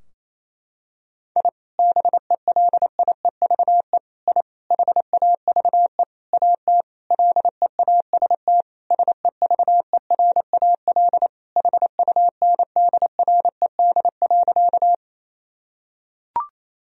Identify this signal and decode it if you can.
Morse code, heard as a keyed tone, about 28 wpm, 700 Hz. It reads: I BELIEVE I HAVE AT LEAST SEVERAL HUNDRED.
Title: Sentences from Top 1000 Words - Review Part 2 - 28wpm